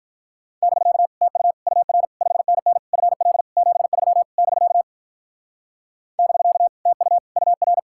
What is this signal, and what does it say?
Morse code, heard as a keyed tone, about 45 wpm, 700 Hz. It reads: <BK> TU UR 5NN FL 73 <BK> <BK> TU UR